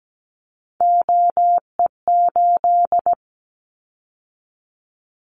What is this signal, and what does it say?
Morse code, heard as a keyed tone, about 17 wpm, 700 Hz. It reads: OE8
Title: Callsign Prefixes for Top 100 DXCC - 17wpm